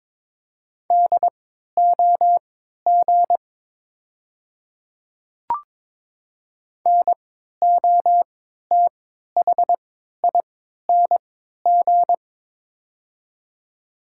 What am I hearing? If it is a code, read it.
Morse code, heard as a keyed tone, about 22 wpm, spaced as for 12 wpm, 700 Hz. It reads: DOG NOTHING
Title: 3x Character Spacing - Top 500 Words - 22wpm